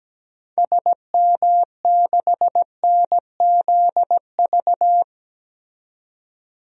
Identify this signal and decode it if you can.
Morse code, heard as a keyed tone, about 17 wpm, 700 Hz. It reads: SM6NZV
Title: DXCC-100 Call Signs - 6 Characters - 17wpm